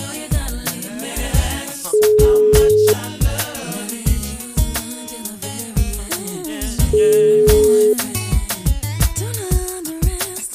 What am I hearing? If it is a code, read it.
Ringback tone (Europe, 425 Hz).